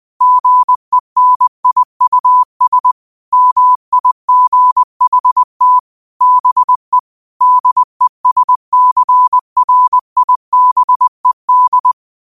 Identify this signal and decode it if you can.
Morse code, heard as a keyed tone, about 20 wpm, 1000 Hz. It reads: GENIUS MIGHT BE DESCRIBED
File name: qotdfiles / 20wpm